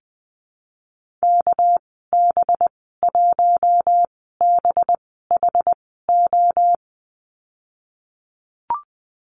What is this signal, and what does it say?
Morse code, heard as a keyed tone, about 19 wpm, spaced as for 14 wpm, 700 Hz. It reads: KB1BHO
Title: 245 Lesson 39g - Letter z - Call Signs 20wpm